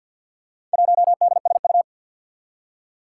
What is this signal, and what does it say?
Morse code, heard as a keyed tone, about 53 wpm, 700 Hz. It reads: 1DSU
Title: Four Challenging Letter-Number Combinations 50wpm